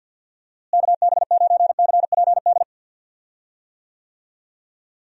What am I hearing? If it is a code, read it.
Morse code, heard as a keyed tone, about 47 wpm, 700 Hz. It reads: KB9CPD
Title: USA Call Signs - 6 Characters - Repeated 3x - 50wpm